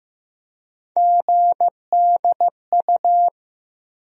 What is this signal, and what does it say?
Morse code, heard as a keyed tone, about 15 wpm, 700 Hz. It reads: GDU
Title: USA Call Sign Postfixes - 3 Letters 15wpm